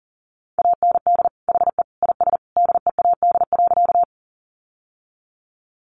Morse code, heard as a keyed tone, about 40 wpm, 700 Hz. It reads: AND HE IS DEAD.
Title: Sentences from Top 1500 Words 40wpm